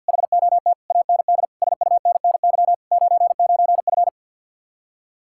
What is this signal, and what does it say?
Morse code, heard as a keyed tone, about 64 wpm, 700 Hz. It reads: HOT AND SUNNY 99F